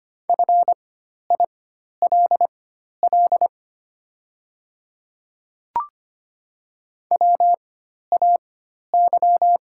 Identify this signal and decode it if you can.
Morse code, heard as a keyed tone, about 25 wpm, spaced as for 12 wpm, 700 Hz. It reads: FILL WAY
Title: 4x Character Spacing - Top 500 Words - 25wpm